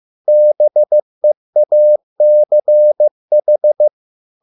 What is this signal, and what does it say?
Morse code, heard as a keyed tone, about 15 wpm, 600 Hz. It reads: BEACH